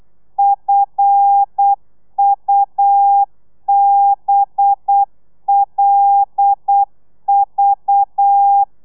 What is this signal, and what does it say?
Morse code, heard as a keyed tone, about 8 wpm, 800 Hz. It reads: FUBLV
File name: telegrafia / i2tfj / Lez5 35